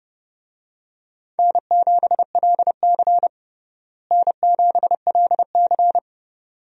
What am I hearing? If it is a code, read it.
Morse code, heard as a keyed tone, about 31 wpm, 700 Hz. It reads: N7LC N7LC
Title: USA Call Signs - 4 Characters - Repeated 2x - 30wpm